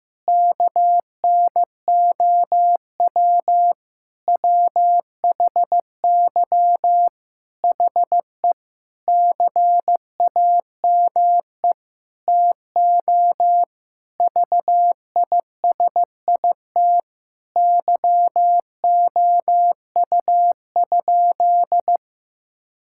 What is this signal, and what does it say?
Morse code, heard as a keyed tone, about 15 wpm, 700 Hz. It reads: KNOW WHY HE CAME TO VISIT YOU?
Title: Sentences from Top 1000 Words - Review Part 6 - 15wpm